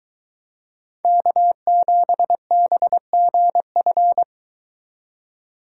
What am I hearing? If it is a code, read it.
Morse code, heard as a keyed tone, about 23 wpm, 700 Hz. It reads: K7BGF